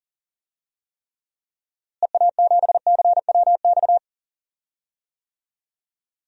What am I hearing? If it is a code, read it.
Morse code, heard as a keyed tone, about 40 wpm, 700 Hz. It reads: EA7CWX
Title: DXCC-100 Call Signs - 6 Characters - Repeated 2x - 40wpm